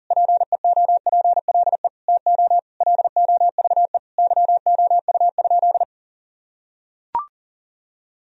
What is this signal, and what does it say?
Morse code, heard as a keyed tone, about 37 wpm, 700 Hz. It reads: PEOPLE TO LOVE YOU?
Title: Sentences from Top 1000 Words - Review Part 3 - 40wpm